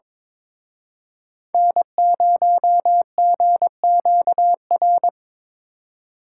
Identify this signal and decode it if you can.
Morse code, heard as a keyed tone, about 25 wpm, 700 Hz. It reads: N0GQR
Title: Speed Racing - USA 1x3 Call Signs 22wpm